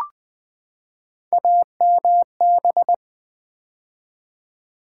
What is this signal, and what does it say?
Morse code, heard as a keyed tone, about 20 wpm, 700 Hz. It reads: AMB
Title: Build A Word - 3 Letters At A Time - 9-Letter Words - 20wpm